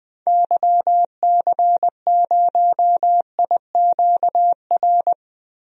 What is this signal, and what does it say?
Morse code, heard as a keyed tone, about 20 wpm, 700 Hz. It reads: YC0IQR
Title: DXCC-100 Call Signs - 6 Characters - Repeated 2x - 20wpm